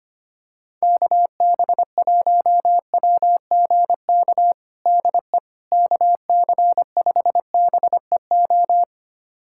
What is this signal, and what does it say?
Morse code, heard as a keyed tone, about 25 wpm, 700 Hz. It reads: KB1WGK DE KC5BEO
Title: USA Callsign DE Callsign - 6 Characters - 25wpm